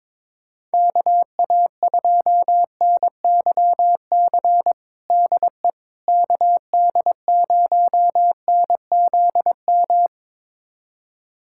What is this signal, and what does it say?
Morse code, heard as a keyed tone, about 22 wpm, 700 Hz. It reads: KA2NYC DE KD0NZM
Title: USA Callsign DE Callsign - 6 Characters - 22wpm